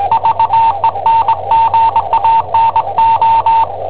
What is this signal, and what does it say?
Morse code, heard as a keyed tone, about 19 wpm, 910 Hz. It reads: VENGANO
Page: Cassetta esercizi preparata da I7OHP per IW5ELP